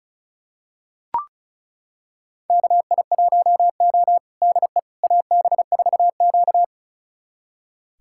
Morse code, heard as a keyed tone, about 36 wpm, 700 Hz. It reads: KI1O DE AB4Q
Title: USA Callsign DE Callsign - 4 Characters - 35wpm